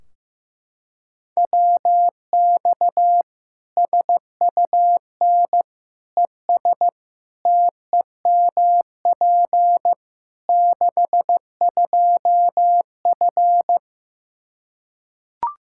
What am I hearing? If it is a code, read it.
Morse code, heard as a keyed tone, about 16 wpm, 700 Hz. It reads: WX SUN ES TEMP 62F
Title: Practice QSO - Exchange 2 - Info - 15wpm